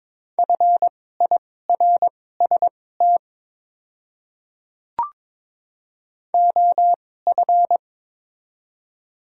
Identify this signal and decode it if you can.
Morse code, heard as a keyed tone, about 22 wpm, spaced as for 16 wpm, 700 Hz. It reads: FIRST OF
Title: 2x Character Spacing - Top 200 Words - 22wpm